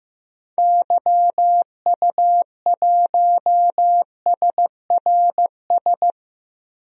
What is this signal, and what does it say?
Morse code, heard as a keyed tone, about 15 wpm, 700 Hz. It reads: YU1SRS
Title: DXCC-100 Call Signs - 6 Characters - Repeated 2x - 15wpm